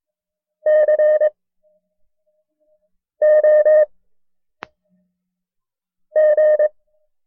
Morse code, heard as a keyed tone, about 22 wpm, spaced as for 4 wpm, 610 Hz. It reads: COG
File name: FreeFiles / CW-practice / KB6MT-tape1-side2